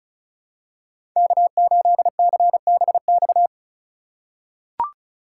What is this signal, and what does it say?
Morse code, heard as a keyed tone, about 34 wpm, 700 Hz. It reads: K8CBX
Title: USA Call Signs - 5 Characters - 2x Character Spacing - 35wpm